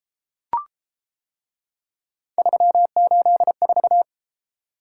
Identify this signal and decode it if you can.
Morse code, heard as a keyed tone, about 31 wpm, 700 Hz. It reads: 384